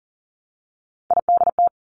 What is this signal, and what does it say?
Morse code, heard as a keyed tone, about 40 wpm, 700 Hz. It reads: IDT
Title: Three Challenging Letter-Number Combinations 40wpm